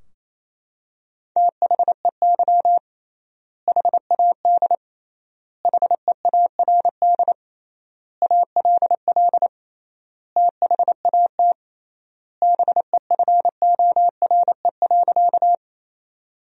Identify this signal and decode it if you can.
Morse code, heard as a keyed tone, about 28 wpm, 700 Hz. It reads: THEY HAD HEARD ALL THAT BEFORE.
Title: 3x Word-Spacing - Sentences from Top 400 Words - 28wpm